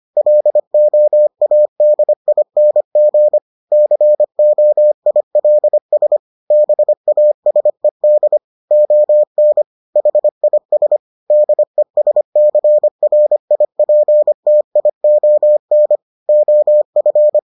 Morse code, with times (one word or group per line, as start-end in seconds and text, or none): 0.2-3.4 LOADING
3.7-6.2 COILS
6.5-8.4 BASED
8.7-9.6 ON
10.0-11.0 HIS
11.3-16.0 DESCRIPTION
16.3-17.4 OF